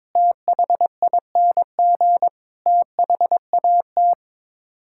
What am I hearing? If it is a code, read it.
Morse code, heard as a keyed tone, about 22 wpm, 700 Hz. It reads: THING THAT
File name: practice / 22wpm / Sets of 7 Words - 22wpm